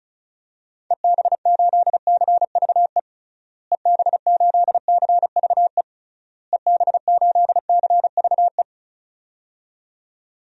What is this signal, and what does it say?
Morse code, heard as a keyed tone, about 36 wpm, 700 Hz. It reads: EB8CVE EB8CVE EB8CVE
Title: DXCC-100 Call Signs - 6 Characters - Repeated 3x - 35wpm